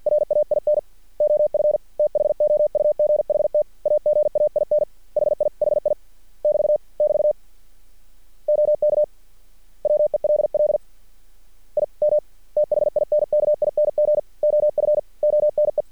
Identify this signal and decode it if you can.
Morse code, heard as a keyed tone, about 50 wpm, 600 Hz. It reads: RAIN OF THOUGHT AGAIN HI HI <BT> <BT> OK WELL I M THINKING OF ONE
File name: archived / CRSnet